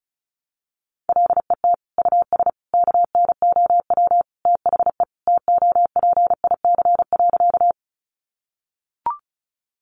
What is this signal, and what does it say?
Morse code, heard as a keyed tone, about 34 wpm, 700 Hz. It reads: LET US KNOW THE TOPIC.
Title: Sentences from Top 1200 Words 35wpm